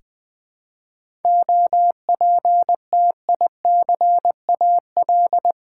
Morse code, heard as a keyed tone, about 20 wpm, 700 Hz. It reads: OPTICAL
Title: 7 Letter Words - First 3 Letters Given - 20wpm